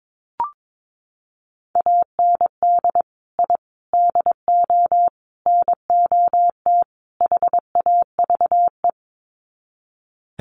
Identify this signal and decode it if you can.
Morse code, heard as a keyed tone, about 22 wpm, 700 Hz. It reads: AND I DO NOT HAVE
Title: Sets of 5 Words Encore 22wpm